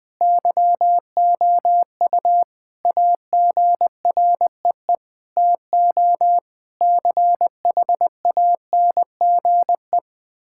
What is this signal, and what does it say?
Morse code, heard as a keyed tone, about 20 wpm, 700 Hz. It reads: YOU AGREE TO CHANGE
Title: Sentences from Top 1000 Words - Review Part 6 - 20wpm